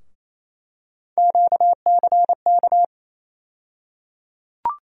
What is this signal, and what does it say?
Morse code, heard as a keyed tone, about 28 wpm, 700 Hz. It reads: QCK
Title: USA Call Sign Postfixes - 3 Letters 28wpm